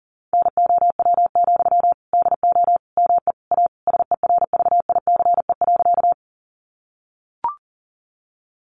Morse code, heard as a keyed tone, about 37 wpm, 700 Hz. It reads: NOW, DO ME A SERVICE.
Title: Sentences from Top 700 Words 40wpm